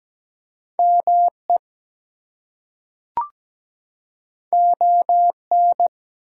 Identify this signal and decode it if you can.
Morse code, heard as a keyed tone, about 17 wpm, 700 Hz. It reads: ME ON
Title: Two Letter Words - 17wpm